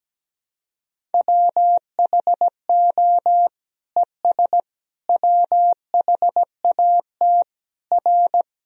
Morse code, heard as a keyed tone, about 17 wpm, 700 Hz. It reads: WHO ES WHAT R
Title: Sentences from Top 100 Words and Abbreviations 17wpm